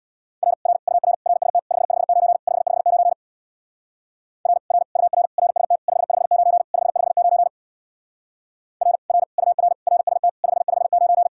Morse code, heard as a keyed tone, about 61 wpm, 700 Hz. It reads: R R UR RST 559 559 R R UR RST 559 559 R R UR RST 559
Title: Speed-Racing Common QSO Phrases - 50wpm